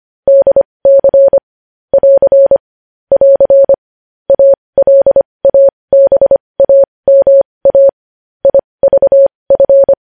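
Morse code, heard as a keyed tone, about 25 wpm, spaced as for 20 wpm, 550 Hz. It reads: DC <AR> <AR> ALABAMA IVF
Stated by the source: Morse code news headlines sent at 20.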